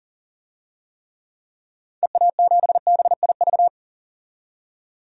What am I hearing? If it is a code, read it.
Morse code, heard as a keyed tone, about 40 wpm, 700 Hz. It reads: EA7BIV
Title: DXCC-100 Call Signs - 6 Characters - 40wpm